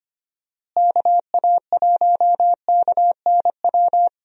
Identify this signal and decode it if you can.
Morse code, heard as a keyed tone, about 25 wpm, 700 Hz. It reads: KA1KNW
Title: USA Call Signs - 6 Characters - Repeated 2x - 25wpm